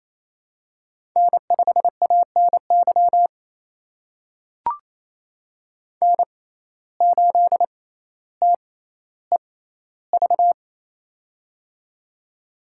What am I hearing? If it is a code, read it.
Morse code, heard as a keyed tone, about 28 wpm, spaced as for 14 wpm, 700 Hz. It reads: N5ANY N8TEV
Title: USA Call Signs - 6x Character Spacing - 28wpm